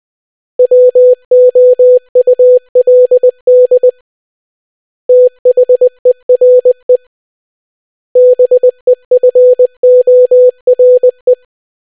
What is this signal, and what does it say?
Morse code, heard as a keyed tone, about 20 wpm, 500 Hz. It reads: WOULD THERE BEFORE